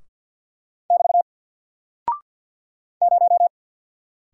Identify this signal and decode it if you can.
Morse code, heard as a keyed tone, about 48 wpm, 700 Hz. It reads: <BT> 0
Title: Single Character (Letter:Number:Punctuation:Prosign) Alt 50wpm